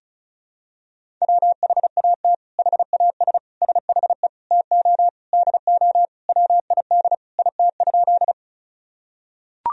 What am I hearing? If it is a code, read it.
Morse code, heard as a keyed tone, about 36 wpm, 700 Hz. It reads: WHAT HAS SHE TO DO WID IT?
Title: Sentences from Top 100 Words and Abbreviations 35wpm